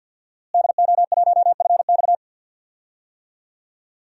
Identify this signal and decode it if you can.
Morse code, heard as a keyed tone, about 51 wpm, 700 Hz. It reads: DO1FX
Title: DXCC-100 Call Signs - 5 Characters - Repeated 3x - 50wpm